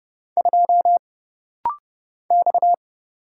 Morse code, heard as a keyed tone, about 30 wpm, 700 Hz. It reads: 2 X
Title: Single Letter-Number Alt - Rapid-Fire 30wpm